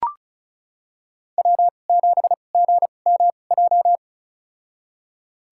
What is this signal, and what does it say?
Morse code, heard as a keyed tone, about 34 wpm, spaced as for 25 wpm, 700 Hz. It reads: W7GMJ
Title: USA Call Signs - 5 Characters - 2x Character Spacing - 35wpm